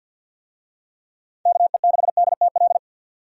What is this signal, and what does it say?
Morse code, heard as a keyed tone, about 52 wpm, 700 Hz. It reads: KE6DTL
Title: USA Call Signs - 6 Characters - 50wpm